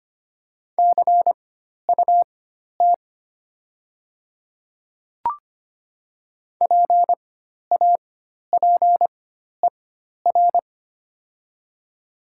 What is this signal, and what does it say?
Morse code, heard as a keyed tone, about 25 wpm, spaced as for 12 wpm, 690 Hz. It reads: CUT PAPER
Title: 4x Character Spacing - Top 300 Words - 25wpm